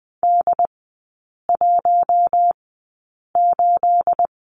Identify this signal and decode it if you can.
Morse code, heard as a keyed tone, about 20 wpm, spaced as for 8 wpm, 700 Hz. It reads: D18
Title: 212 Lesson 34e - Number 8 - Three Characters 20wpm